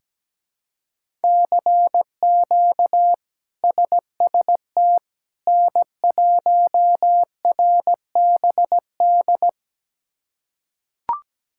Morse code, heard as a keyed tone, about 17 wpm, 700 Hz. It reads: CQ SST N1RBD
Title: K1USN SST QSO Practice - 17wpm